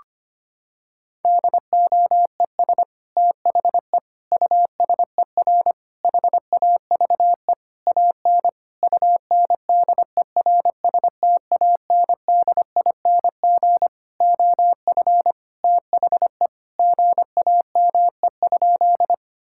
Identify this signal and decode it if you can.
Morse code, heard as a keyed tone, about 25 wpm, 700 Hz. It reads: DOES THE USER HAVE AN UNDERSTANDING OF THE GAME?
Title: Sentences from Top 1000 Words - Review Part 5 - 25wpm